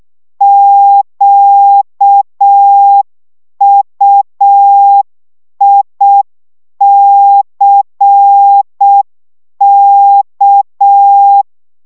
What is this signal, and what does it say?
Morse code, heard as a keyed tone, about 6 wpm, 800 Hz. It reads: QUICK
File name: szamolo / CW 5 baud